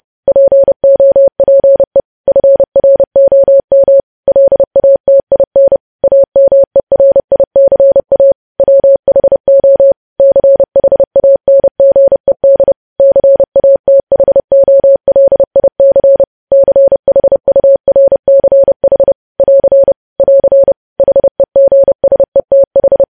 Morse code, heard as a keyed tone, about 30 wpm, 550 Hz. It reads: POPE FROM LATIN AMERICA WHO CHANGED CATHOLIC CHURCH <AR> <AR> HEGSETH
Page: News Headlines in Morse Code at 30 WPM